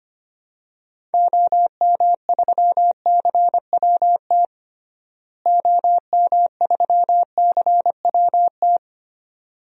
Morse code, heard as a keyed tone, about 25 wpm, 700 Hz. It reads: OM3CWT OM3CWT
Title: DXCC-100 Call Signs - 6 Characters - Repeated 2x - 25wpm